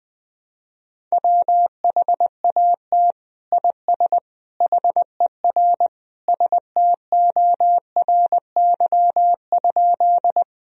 Morse code, heard as a keyed tone, about 20 wpm, 700 Hz. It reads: WHAT IS HER STORY?
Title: Sentences from Top 1000 Words - Review Part 1 - 20wpm